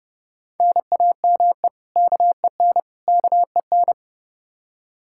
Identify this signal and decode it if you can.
Morse code, heard as a keyed tone, about 30 wpm, 700 Hz. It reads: NAME KEN KEN
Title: Practice QSO - Exchange 1 - Beginning - 30wpm